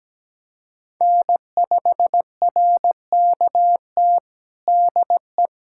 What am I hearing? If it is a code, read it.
Morse code, heard as a keyed tone, about 17 wpm, 700 Hz. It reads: N5RKT DE
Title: USA Callsign DE Callsign - 5 Characters - 17wpm